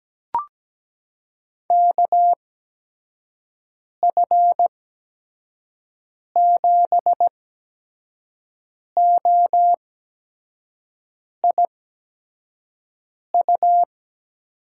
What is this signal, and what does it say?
Morse code, heard as a keyed tone, about 17 wpm, spaced as for 5 wpm, 700 Hz. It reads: KF7OIU
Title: USA Call Signs - 6 Characters - 8x Character Spacing - 17wpm